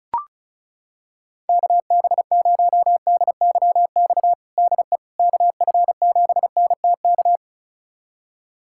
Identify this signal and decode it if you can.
Morse code, heard as a keyed tone, about 34 wpm, 700 Hz. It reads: KB0DYX DE KF7NTK
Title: USA Callsign DE Callsign - 6 Characters - 35wpm